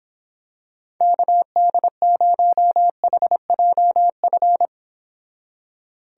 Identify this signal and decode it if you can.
Morse code, heard as a keyed tone, about 26 wpm, 700 Hz. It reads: KD0HJF